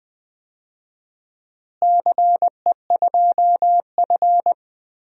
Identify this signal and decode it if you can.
Morse code, heard as a keyed tone, about 20 wpm, 700 Hz. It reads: CE2F
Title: DXCC-100 Call Signs - 4 Characters - Repeated 3x - 20wpm